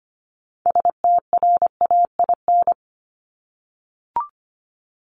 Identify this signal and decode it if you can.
Morse code, heard as a keyed tone, about 25 wpm, 700 Hz. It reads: STRAIN
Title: Top 1751-2000 Words 25wpm